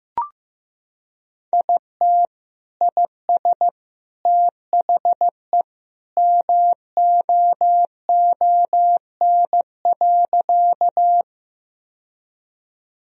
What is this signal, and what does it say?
Morse code, heard as a keyed tone, about 15 wpm, 700 Hz. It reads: IT IS THE MOON.
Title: Sentences from Top 1000 Words - Review Part 3 - 15wpm